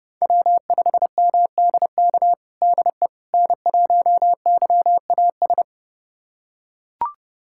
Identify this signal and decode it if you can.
Morse code, heard as a keyed tone, about 30 wpm, 700 Hz. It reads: W5MDK DE N1YAS
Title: USA Callsign DE Callsign - 5 Characters - 30wpm